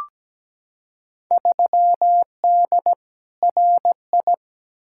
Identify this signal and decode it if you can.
Morse code, heard as a keyed tone, about 17 wpm, 700 Hz. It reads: 3D RI
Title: Field Day Exchanges Encore - 17wpm